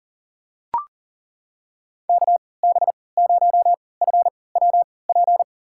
Morse code, heard as a keyed tone, about 40 wpm, spaced as for 23 wpm, 700 Hz. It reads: KB0FWP
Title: USA Call Signs - 6 Characters - 3x Character Spacing - 40wpm